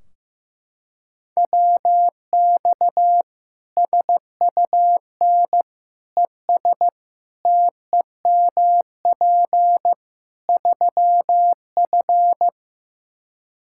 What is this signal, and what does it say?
Morse code, heard as a keyed tone, about 16 wpm, 700 Hz. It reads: WX SUN ES TEMP 3F
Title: Practice QSO - Exchange 2 - Info - 15wpm